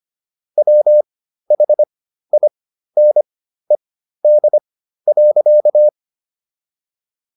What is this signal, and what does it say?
Morse code, heard as a keyed tone, about 25 wpm, spaced as for 13 wpm, 600 Hz. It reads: WHINED.